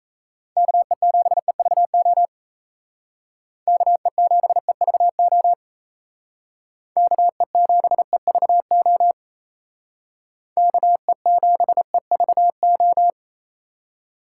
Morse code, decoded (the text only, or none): KE7EVO KE7EVO KE7EVO KE7EVO